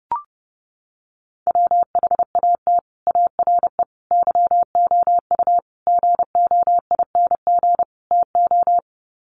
Morse code, heard as a keyed tone, about 30 wpm, 700 Hz. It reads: WHAT ARE YOU GOING TO
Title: Sets of 5 Words 30wpm